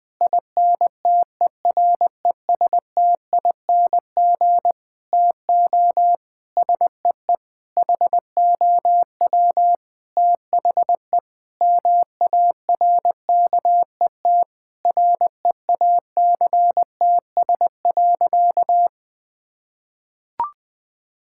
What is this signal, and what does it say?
Morse code, heard as a keyed tone, about 20 wpm, 700 Hz. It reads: INTERESTING TO SEE HOW THE MARKET REACTS.
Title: Sentences with 8-Word Phrases - 20wpm